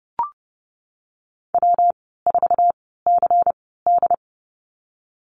Morse code, heard as a keyed tone, about 30 wpm, 700 Hz. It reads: W 4 C D
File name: Lesson 19g - Letter w - Call Signs 30wpm